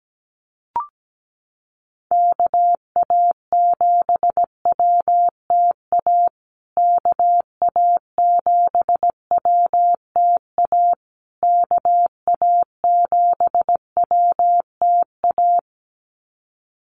Morse code, heard as a keyed tone, about 17 wpm, 700 Hz. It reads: KA7WTA KA7WTA KA7WTA
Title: USA Call Signs Repeated 3x 17wpm